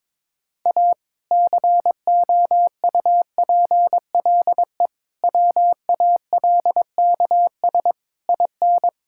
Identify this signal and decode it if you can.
Morse code, heard as a keyed tone, about 21 wpm, 700 Hz. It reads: A COUPLE WALKS IN
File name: practice / 22wpm / Sentences with 2-Word Phrases - 22wpm